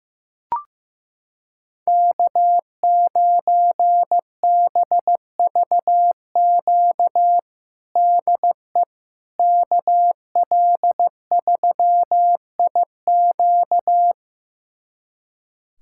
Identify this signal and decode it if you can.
Morse code, heard as a keyed tone, about 15 wpm, 700 Hz. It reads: K9BVQ DE KL3IQ
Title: USA Callsign DE Callsign - 5 Characters - 15wpm